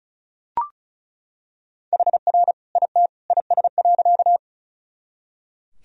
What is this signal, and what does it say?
Morse code, heard as a keyed tone, about 35 wpm, 700 Hz. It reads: HR IT IS.
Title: Sentences from Top 200 Words and Abbreviations 35wpm